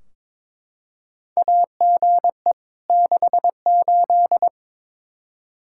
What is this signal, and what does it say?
Morse code, heard as a keyed tone, about 22 wpm, 700 Hz. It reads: AGE 68
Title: Practice QSO - Exchange 3 - More Info - 22wpm